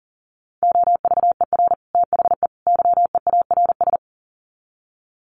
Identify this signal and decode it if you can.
Morse code, heard as a keyed tone, about 40 wpm, 700 Hz. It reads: OVER THE YEARS
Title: Sets of 3 Words 40wpm